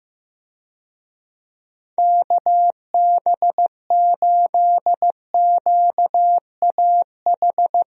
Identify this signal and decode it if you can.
Morse code, heard as a keyed tone, about 15 wpm, 700 Hz. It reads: KB8QAH
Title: USA Call Signs - 6 Characters - Repeated 2x - 15wpm